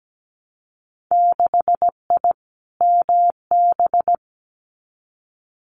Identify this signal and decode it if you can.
Morse code, heard as a keyed tone, about 17 wpm, 700 Hz. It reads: 6I MB
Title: Winter Field Day Exchanges Encore 17wpm